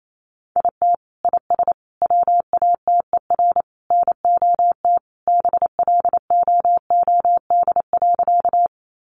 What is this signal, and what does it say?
Morse code, heard as a keyed tone, about 28 wpm, 700 Hz. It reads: IT IS WATER NOT BLOOD.
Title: Sentences from Top 1100 Words 28wpm